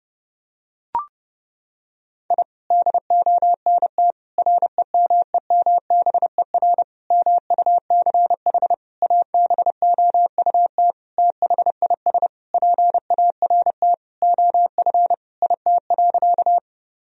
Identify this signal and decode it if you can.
Morse code, heard as a keyed tone, about 30 wpm, 700 Hz. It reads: I DONT REMEMBER MUCH ABOUT THIS PART OF IT.
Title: Sentences from Top 1000 Words - Review Part 1 - 30wpm